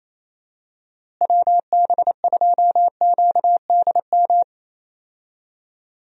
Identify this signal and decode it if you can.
Morse code, heard as a keyed tone, about 27 wpm, 700 Hz. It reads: WB2QDM
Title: USA Call Signs - 6 Characters - Repeated 3x - 28wpm